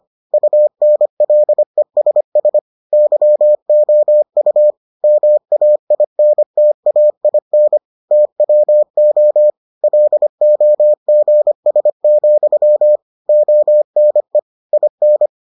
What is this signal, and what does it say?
Morse code, heard as a keyed tone, about 25 wpm, 600 Hz. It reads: UNLESS YOU MAINTAIN TWO LOGS, ONE IN